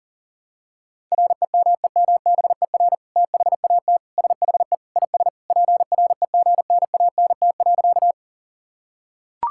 Morse code, heard as a keyed tone, about 37 wpm, 700 Hz. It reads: REMEMBER THAT SHE IS PREGNANT.
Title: Sentences from Top 1000 Words - Review Part 6 - 40wpm